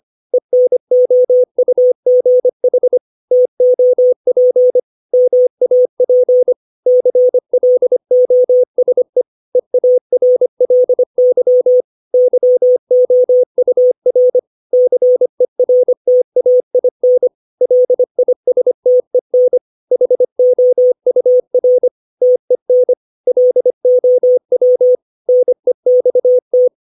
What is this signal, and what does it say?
Morse code, heard as a keyed tone, about 25 wpm, 500 Hz. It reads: ENOUGH TOP MAP CLOSE EARLY YOUR CERTAIN LISTEN HOUR TEN LOW NEXT